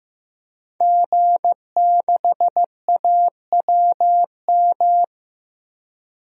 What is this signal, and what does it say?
Morse code, heard as a keyed tone, about 15 wpm, 700 Hz. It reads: G6AWM
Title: DXCC-100 Call Signs - 5 Characters - 15wpm